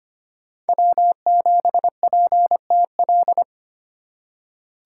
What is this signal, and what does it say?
Morse code, heard as a keyed tone, about 25 wpm, 700 Hz. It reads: W7PTL
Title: USA Call Signs - 5 Characters - 2x Character Spacing - 25wpm